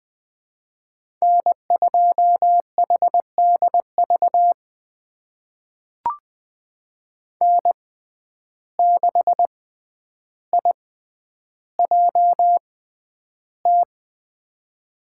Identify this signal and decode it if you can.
Morse code, heard as a keyed tone, about 20 wpm, spaced as for 10 wpm, 700 Hz. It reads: N2HDV N6IJT
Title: USA Call Signs - 5 Characters - 6x Character Spacing - 20wpm